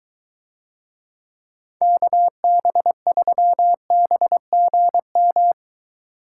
Morse code, heard as a keyed tone, about 22 wpm, 700 Hz. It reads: KB3BGM